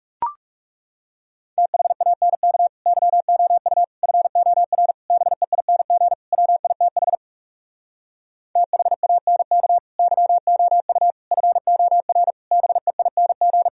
Morse code, heard as a keyed tone, about 38 wpm, 700 Hz. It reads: THANK YOU FOR BEING WITH THANK YOU FOR BEING